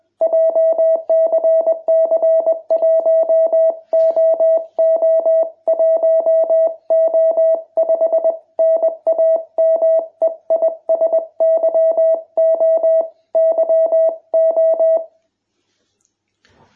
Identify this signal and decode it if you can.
Morse code, heard as a keyed tone, about 21 wpm, 650 Hz. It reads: JCC1OO1O5 NAME IS YO YO